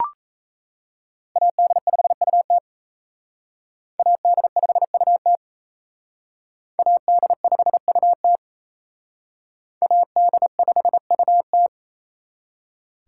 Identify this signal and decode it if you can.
Morse code, heard as a keyed tone, about 36 wpm, 700 Hz. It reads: AD5UT AD5UT AD5UT AD5UT